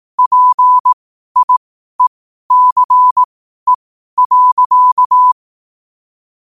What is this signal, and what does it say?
Morse code, heard as a keyed tone, about 18 wpm, spaced as for 12 wpm, 1000 Hz. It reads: PIECE.
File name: 12wpm